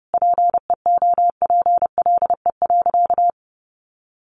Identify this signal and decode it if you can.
Morse code, heard as a keyed tone, about 30 wpm, 700 Hz. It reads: PEOPLE.
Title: Sentences from Top 500 Words 30wpm